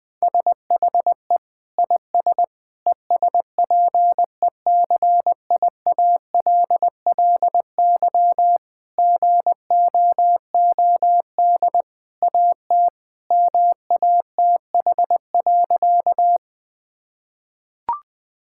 Morse code, heard as a keyed tone, about 20 wpm, 700 Hz. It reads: SHE IS ESPECIALLY GOOD AT MATH.
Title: Sentences from Top 1000 Words - Review Part 6 - 20wpm